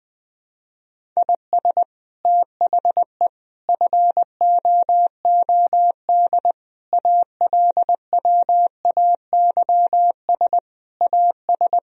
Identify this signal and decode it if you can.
Morse code, heard as a keyed tone, about 20 wpm, 700 Hz. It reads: IS THE FOOD ALWAYS AS
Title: Sentences from Top 1000 Words - Review Part 1 - 20wpm